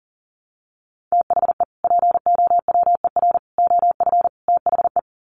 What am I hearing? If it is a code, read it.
Morse code, heard as a keyed tone, about 40 wpm, 700 Hz. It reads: THE POWER OF THE
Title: Sets of 4 Words 40wpm